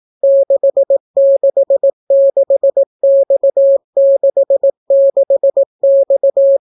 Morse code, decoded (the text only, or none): 666X66X